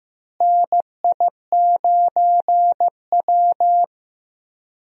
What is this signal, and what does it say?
Morse code, heard as a keyed tone, about 15 wpm, 700 Hz. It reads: NI9W
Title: USA Call Signs - 4 Characters - Repeated 3x - 15wpm